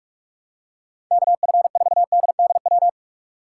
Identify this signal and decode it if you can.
Morse code, heard as a keyed tone, about 46 wpm, 700 Hz. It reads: KF4DDW
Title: USA Call Signs - 6 Characters - 2x Character Spacing - 45wpm